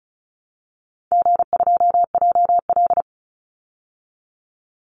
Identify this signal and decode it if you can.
Morse code, heard as a keyed tone, about 34 wpm, 700 Hz. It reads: G2JL
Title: Easy Call Signs Repeated 2x - 35wpm